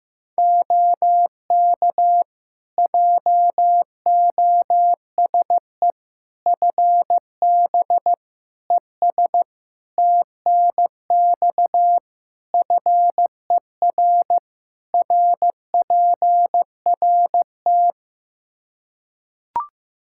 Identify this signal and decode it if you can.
Morse code, heard as a keyed tone, about 15 wpm, 700 Hz. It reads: OK JOSE FB ES TNX FER RPRT
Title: K1USN SST QSO Practice 15wpm